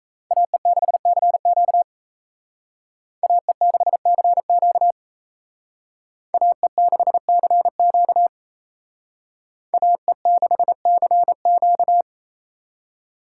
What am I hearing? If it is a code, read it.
Morse code, heard as a keyed tone, about 35 wpm, 700 Hz. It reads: AE6CQ AE6CQ AE6CQ AE6CQ